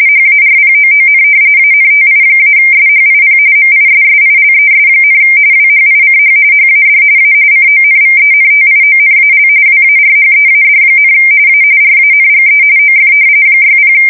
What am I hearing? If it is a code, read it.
RTTY, about 45.45 baud. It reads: HARPOON ANTI-SHIP MISSILES, WHICH HAD A CONVENTIONAL WARHEAD, A RANGE OF 60
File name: fsk170 / battleshipnj